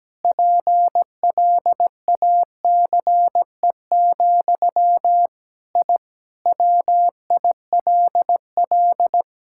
Morse code, heard as a keyed tone, about 17 wpm, 700 Hz. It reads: PLACE, I WILL
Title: Sentences from Top 1000 Words - Review Part 5 - 17wpm